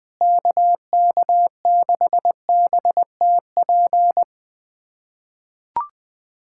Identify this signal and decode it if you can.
Morse code, heard as a keyed tone, about 20 wpm, 700 Hz. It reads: KK6BTP
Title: USA Callsign DE Callsign 20wpm